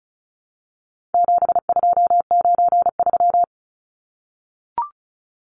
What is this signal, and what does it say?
Morse code, heard as a keyed tone, about 34 wpm, 700 Hz. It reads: 7293
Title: Numbers - 4-Digit 35wpm